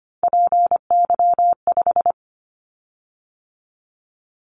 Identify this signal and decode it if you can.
Morse code, heard as a keyed tone, about 25 wpm, 700 Hz. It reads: PY5
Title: Callsign Prefixes for Top 100 DXCC - 25wpm